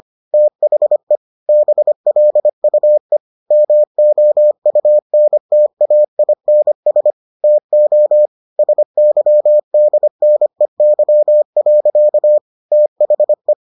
Morse code, heard as a keyed tone, about 25 wpm, 600 Hz. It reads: THE BLUE MOUNTAINS TO SYDNEY. THE